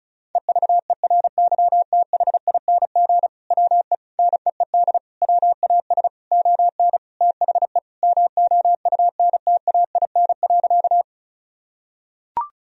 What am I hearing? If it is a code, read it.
Morse code, heard as a keyed tone, about 36 wpm, 700 Hz. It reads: EVERYTHING WE NEED WAS ON THE MOUNTAIN.
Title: Sentences from Top 1000 Words - Review Part 1 - 35wpm